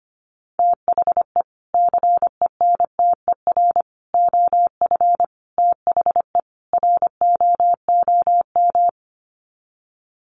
Morse code, heard as a keyed tone, about 25 wpm, 700 Hz. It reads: THE CENTER OF THE ROOM
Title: Sets of 5 Words Encore 25wpm